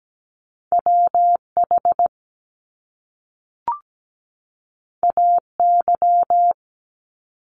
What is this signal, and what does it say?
Morse code, heard as a keyed tone, about 17 wpm, 700 Hz. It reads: WH AY
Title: Top Two Letter Combinations in Words 17wpm